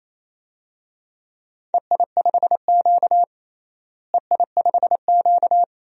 Morse code, heard as a keyed tone, about 28 wpm, 700 Hz. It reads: EI5Q EI5Q
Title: DXCC-100 Call Signs - 4 Characters - Repeated 3x - 28wpm